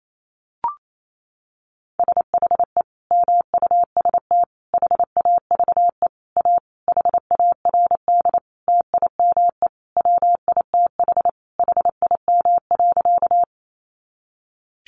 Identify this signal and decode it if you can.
Morse code, heard as a keyed tone, about 28 wpm, 700 Hz. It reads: SHE MUST HAVE A HARD TIME WITH HIM.
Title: Sentences from Top 300 Words 28wpm